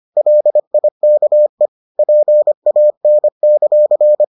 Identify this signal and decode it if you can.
Morse code, heard as a keyed tone, about 25 wpm, 600 Hz. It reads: LIKE PAN;